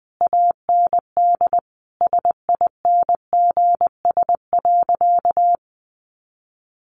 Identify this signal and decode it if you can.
Morse code, heard as a keyed tone, about 20 wpm, 700 Hz. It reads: AND SINGS.
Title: Easy Phrases 20wpm